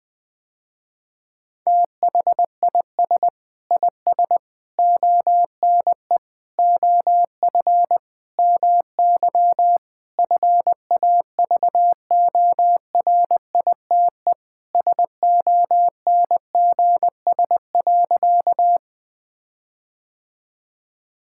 Morse code, heard as a keyed tone, about 20 wpm, 700 Hz. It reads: THIS IS ONE OF MY FAVORITE SONGS.
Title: Sentences with 4-Word Phrases - Spoken Beginning - 20wpm